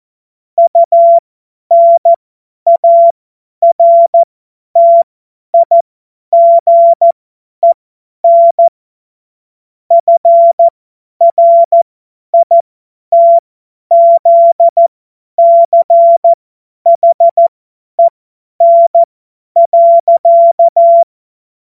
Morse code, heard as a keyed tone, about 14 wpm, spaced as for 10 wpm, 680 Hz. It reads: UNARTIGEN FRITZCHEN.
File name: Morsekurs / Witze / Witz 15. August 2023 50 F70